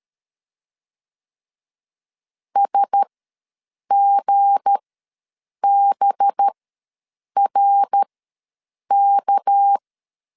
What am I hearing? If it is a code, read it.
Morse code, heard as a keyed tone, about 13 wpm, spaced as for 7 wpm, 800 Hz. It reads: SGBRK
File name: LEZIONI / 7 GF9Z / 7 G ws 12wpm 3140